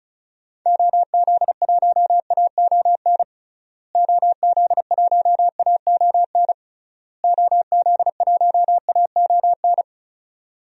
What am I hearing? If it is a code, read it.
Morse code, heard as a keyed tone, about 34 wpm, 700 Hz. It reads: OZ1AON OZ1AON OZ1AON
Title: DXCC-100 Call Signs - 6 Characters - Repeated 3x - 35wpm